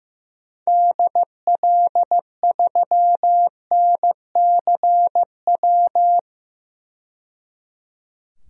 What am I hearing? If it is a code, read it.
Morse code, heard as a keyed tone, about 15 wpm, 700 Hz. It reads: DL3NCW
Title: DXCC-100 Call Signs - 6 Characters - 15wpm